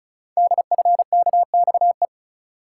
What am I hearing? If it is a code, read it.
Morse code, heard as a keyed tone, about 36 wpm, 690 Hz. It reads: DFKXE
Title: Code Groups - 5 Letters - 35wpm